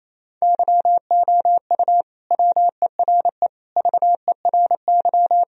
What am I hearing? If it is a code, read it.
Morse code, heard as a keyed tone, about 28 wpm, 700 Hz. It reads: YOU WERE VERY
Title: Sentences from Top 1000 Words - Review Part 4 - 28wpm